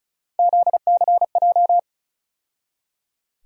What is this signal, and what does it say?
Morse code, heard as a keyed tone, about 36 wpm, 700 Hz. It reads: ZCJ
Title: USA Call Sign Postfixes - 3 Letters 35wpm